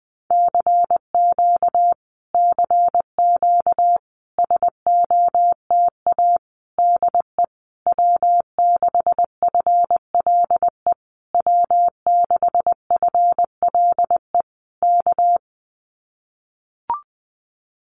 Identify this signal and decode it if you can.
Morse code, heard as a keyed tone, about 20 wpm, 700 Hz. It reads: CQ CQ SOTA DE W6FLE W6FLE K
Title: Common QSO Phrases Encore 20wpm